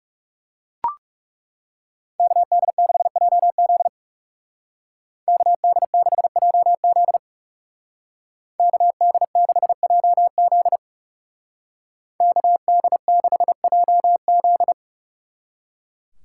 Morse code, heard as a keyed tone, about 36 wpm, 700 Hz. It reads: KD6JZ KD6JZ KD6JZ KD6JZ
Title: Speed Racing - USA 2x2 Call Signs 30wpm